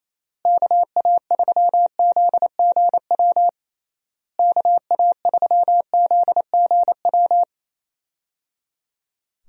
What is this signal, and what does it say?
Morse code, heard as a keyed tone, about 28 wpm, 700 Hz. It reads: KA3ZGW KA3ZGW
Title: USA Call Signs - 6 Characters - Repeated 3x - 28wpm